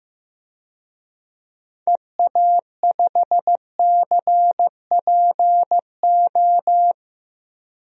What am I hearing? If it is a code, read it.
Morse code, heard as a keyed tone, about 15 wpm, 700 Hz. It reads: EA5CPO
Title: DXCC-100 Call Signs - 6 Characters - Repeated 2x - 15wpm